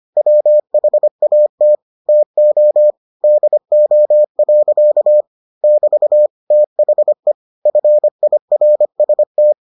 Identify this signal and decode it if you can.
Morse code, heard as a keyed tone, about 25 wpm, 600 Hz. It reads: WHAT TO DO. <BT> THE FIRST